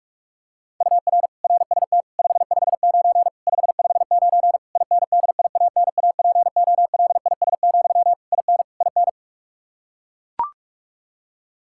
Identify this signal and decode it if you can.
Morse code, heard as a keyed tone, about 41 wpm, 700 Hz. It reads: UR RST 559 559 INDIANAPOLIS, IN IN
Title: Practice QSO - Exchange 1 - Beginning - 45wpm